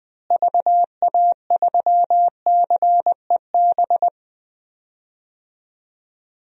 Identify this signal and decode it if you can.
Morse code, heard as a keyed tone, about 20 wpm, 700 Hz. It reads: VA3CEB
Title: DXCC-100 Call Signs - 6 Characters - Repeated 2x - 20wpm